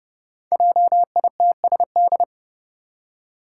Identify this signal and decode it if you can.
Morse code, heard as a keyed tone, about 30 wpm, 700 Hz. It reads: JITSD
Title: Five Challenging Letter-Number Combinations 30wpm